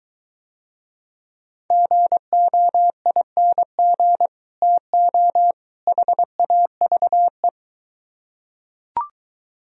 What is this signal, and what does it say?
Morse code, heard as a keyed tone, about 23 wpm, 700 Hz. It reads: GOING TO HAVE